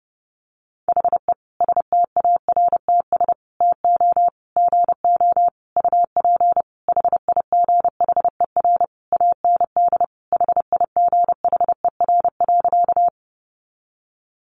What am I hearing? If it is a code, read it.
Morse code, heard as a keyed tone, about 30 wpm, 700 Hz. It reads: HE STARTS TO GO UP HIGHER AND HIGHER.
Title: Additional Easy Phrases 30wpm